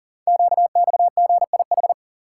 Morse code, heard as a keyed tone, about 38 wpm, 700 Hz. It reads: QXGIH
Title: Code Groups - 5 Letters - 40wpm